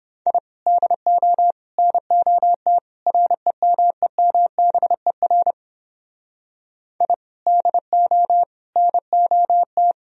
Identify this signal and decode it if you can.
Morse code, heard as a keyed tone, about 29 wpm, 700 Hz. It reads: I DO NOT REMEMBER I DO NOT